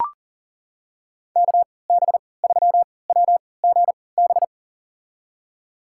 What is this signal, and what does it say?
Morse code, heard as a keyed tone, about 39 wpm, spaced as for 22 wpm, 700 Hz. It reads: KB3WGB
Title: USA Call Signs - 6 Characters - 3x Character Spacing - 40wpm